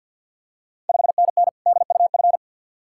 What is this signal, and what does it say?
Morse code, heard as a keyed tone, about 51 wpm, 690 Hz. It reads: 5NN DUF
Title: Ontario QSO Party Exchange 50wpm